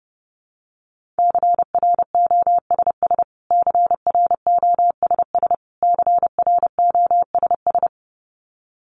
Morse code, heard as a keyed tone, about 30 wpm, 700 Hz. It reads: CROSS CROSS CROSS
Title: Top 500 Words Repeated 30wpm